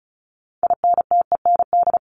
Morse code, heard as a keyed tone, about 34 wpm, 700 Hz. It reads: INTEND
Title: Top 1501-1750 Words 35wpm